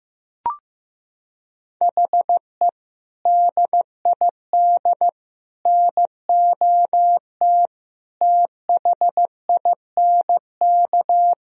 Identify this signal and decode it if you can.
Morse code, heard as a keyed tone, about 15 wpm, 700 Hz. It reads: HE DID NOT THINK
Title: Sentences from Top 1000 Words - Review Part 3 - 15wpm